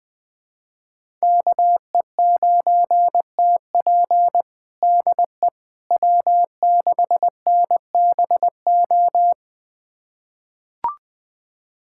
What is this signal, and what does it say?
Morse code, heard as a keyed tone, about 20 wpm, 700 Hz. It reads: KE9TP DE W6NBO
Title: USA Callsign DE Callsign - 5 Characters - 20wpm